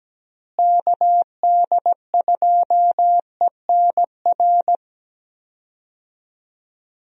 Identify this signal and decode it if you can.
Morse code, heard as a keyed tone, about 17 wpm, 700 Hz. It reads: KD2ENR
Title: USA Call Signs - 6 Characters - Repeated 2x - 17wpm